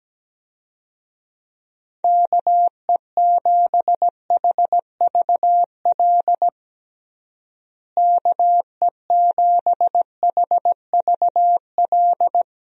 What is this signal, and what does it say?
Morse code, heard as a keyed tone, about 17 wpm, 700 Hz. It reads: KE7HVL KE7HVL
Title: USA Call Signs - 6 Characters - Repeated 2x - 17wpm